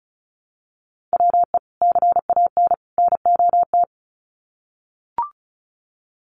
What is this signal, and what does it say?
Morse code, heard as a keyed tone, about 34 wpm, 700 Hz. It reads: WE CAN NOT
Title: Sets of 3 Words 35wpm